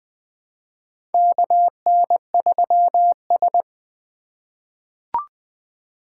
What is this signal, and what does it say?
Morse code, heard as a keyed tone, about 20 wpm, 700 Hz. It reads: KN3S
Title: USA Call Signs - 2x Character Spacing - 20wpm